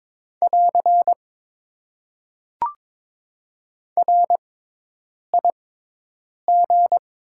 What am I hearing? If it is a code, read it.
Morse code, heard as a keyed tone, about 21 wpm, spaced as for 8 wpm, 700 Hz. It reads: <AR> RIG
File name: Top 90 QSO Elements - 6x Character Spacing - 22wpm